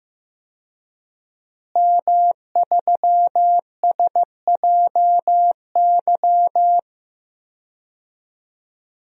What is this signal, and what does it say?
Morse code, heard as a keyed tone, about 15 wpm, 700 Hz. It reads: M3SJY
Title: DXCC-100 Call Signs - 5 Characters - Repeated 3x - 15wpm